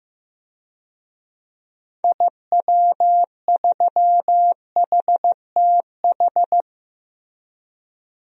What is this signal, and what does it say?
Morse code, heard as a keyed tone, about 16 wpm, 700 Hz. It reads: IW3HTH
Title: DXCC-100 Call Signs - 6 Characters - Repeated 3x - 15wpm